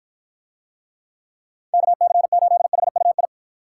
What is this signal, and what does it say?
Morse code, heard as a keyed tone, about 52 wpm, 700 Hz. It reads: KC8HUI